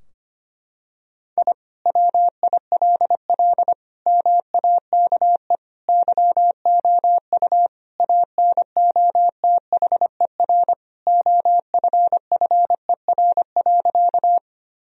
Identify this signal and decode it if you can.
Morse code, heard as a keyed tone, about 25 wpm, 700 Hz. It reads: I WILL MAKE YOU ANOTHER OFFER.
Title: Sentences from Top 1000 Words - Review Part 4 - 25wpm